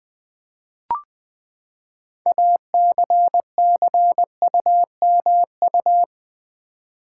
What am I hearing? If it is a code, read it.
Morse code, heard as a keyed tone, about 20 wpm, 700 Hz. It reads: ACCUMU
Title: Build A Word - 6 Letters At A Time - 12-Letter Words - 20wpm